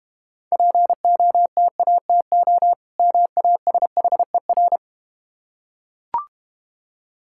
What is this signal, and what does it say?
Morse code, heard as a keyed tone, about 30 wpm, 700 Hz. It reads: POTATO MASHER
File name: Easy Words - Speed-Racing with Hints - 25wpm